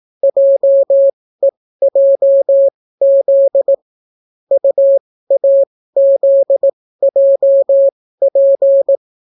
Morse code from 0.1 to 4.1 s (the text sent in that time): JEJZ